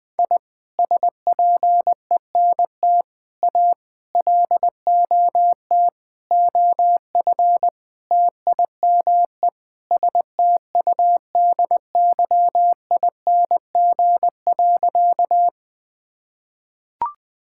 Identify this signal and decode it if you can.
Morse code, heard as a keyed tone, about 21 wpm, 700 Hz. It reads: I SPENT A LOT OF TIME STUDYING.
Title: Sentences with 6-Word Phrases - Spoken Beginning - 20wpm